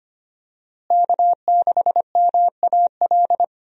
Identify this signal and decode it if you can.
Morse code, heard as a keyed tone, about 25 wpm, 700 Hz. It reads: K6MAL
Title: USA Call Signs - 5 Characters - Repeated 3x - 25wpm